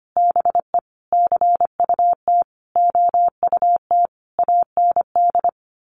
Morse code, heard as a keyed tone, about 25 wpm, 700 Hz. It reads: BE CUT OUT AND
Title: A Sound Of Thunder 25wpm